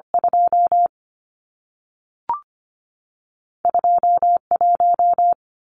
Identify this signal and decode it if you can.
Morse code, heard as a keyed tone, about 25 wpm, 700 Hz. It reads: 2 21
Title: Numbers - 2-Digit 25wpm